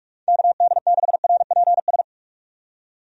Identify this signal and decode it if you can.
Morse code, heard as a keyed tone, about 46 wpm, 700 Hz. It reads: KD6RPS
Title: USA Call Signs - 6 Characters - 45wpm